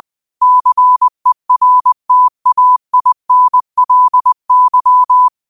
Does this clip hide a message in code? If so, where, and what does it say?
Yes: Morse code from 0.4 to 5.3 s, heard as a keyed tone, about 20 wpm, 1000 Hz. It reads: CERTAINLY